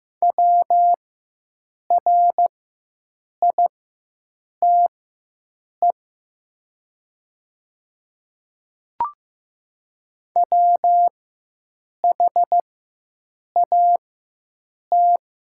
Morse code, heard as a keyed tone, about 15 wpm, spaced as for 7 wpm, 700 Hz. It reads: WRITE WHAT
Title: 4x Character Spacing - Top 200 Words - 15wpm